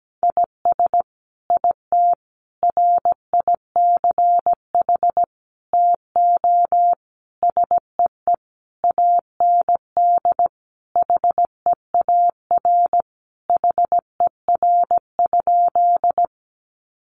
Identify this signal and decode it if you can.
Morse code, heard as a keyed tone, about 17 wpm, 700 Hz. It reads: IS IT RICH TO SEE AND HEAR HER?
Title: Sentences from Top 1100 Words 17wpm